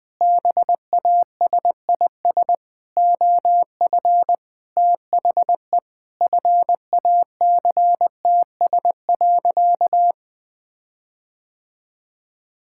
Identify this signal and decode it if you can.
Morse code, heard as a keyed tone, about 20 wpm, 700 Hz. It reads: BASIS OF THE FACTS.
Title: Sentences with 5-Word Phrases - 20wpm